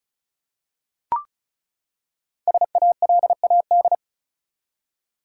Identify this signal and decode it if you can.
Morse code, heard as a keyed tone, about 36 wpm, 700 Hz. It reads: SALAD
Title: Five Letter Words - 35wpm